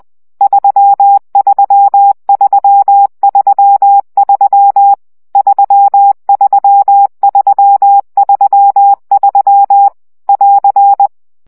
Morse code, heard as a keyed tone, about 20 wpm, 800 Hz. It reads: 33333 33333 <AR>
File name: CW gyakorlas 3 17 baud